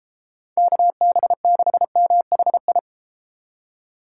Morse code, heard as a keyed tone, about 30 wpm, 700 Hz. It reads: KB6MHI